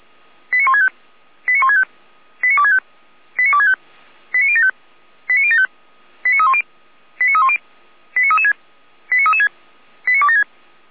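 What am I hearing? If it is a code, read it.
Five-tone selective call (ZVEI), heard as quick runs of five tones, tones of about 80 ms. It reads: 89165 89165 89265 89265 89075 89075 89310 89310 89336 89336 89276
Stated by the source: ZVEI five-tone call received by SDR.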